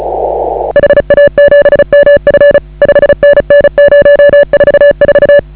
Morse code, heard as a keyed tone, about 34 wpm, 1700 Hz, with noise. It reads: HA7MF 5NN 044
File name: AUDIO2019 / HA8KAZ CW